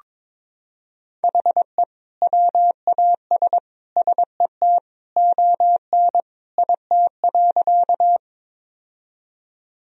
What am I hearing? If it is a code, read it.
Morse code, heard as a keyed tone, about 22 wpm, 700 Hz. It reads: HE WAS SET ON IT.
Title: Sentences from Top 1000 Words - Review Part 1 - 22wpm